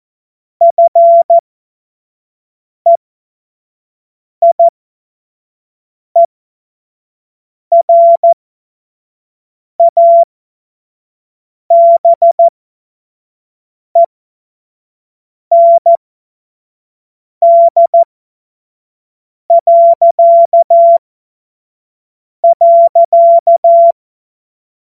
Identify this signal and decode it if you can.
Morse code, heard as a keyed tone, about 14 wpm, spaced as for 5 wpm, 680 Hz. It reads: FEIERABEND..
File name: Morsekurs / Witze / Witz 08. November 2022 25 F70